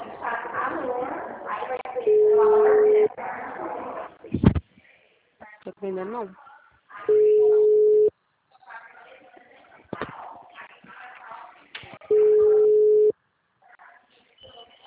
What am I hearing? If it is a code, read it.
Ringback tone (Europe, 425 Hz), with speech.
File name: out- -2026- - - .26024